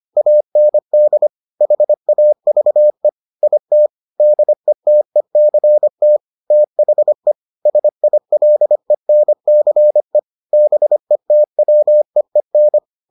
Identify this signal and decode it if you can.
Morse code, heard as a keyed tone, about 25 wpm, 600 Hz. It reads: AND HAVE IT DETECT THE SILENCE BETWEEN